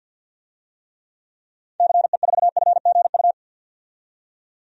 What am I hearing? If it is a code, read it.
Morse code, heard as a keyed tone, about 52 wpm, 700 Hz. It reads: CE4FGU
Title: DXCC-100 Call Signs - 6 Characters - 50wpm